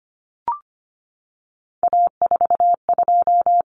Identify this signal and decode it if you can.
Morse code, heard as a keyed tone, about 25 wpm, 700 Hz. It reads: A42
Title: Common Cut Numbers - 3-Digit 25wpm